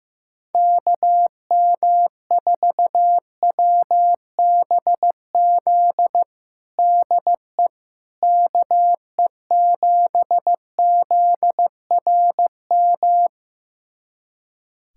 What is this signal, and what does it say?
Morse code, heard as a keyed tone, about 16 wpm, 700 Hz. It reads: KM4WBZ DE KE7ZRM
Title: USA Callsign DE Callsign 15wpm